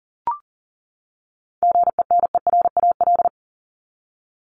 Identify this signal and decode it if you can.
Morse code, heard as a keyed tone, about 40 wpm, 700 Hz. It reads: GENERAL
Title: LY Suffix - Top 100 Words with Base Words 40wpm